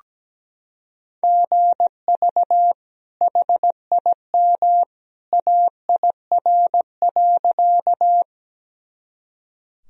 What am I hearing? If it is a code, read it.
Morse code, heard as a keyed tone, about 17 wpm, 700 Hz. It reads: GV HIM AIR.
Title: Sentences from Top 200 Words and Abbreviations 17wpm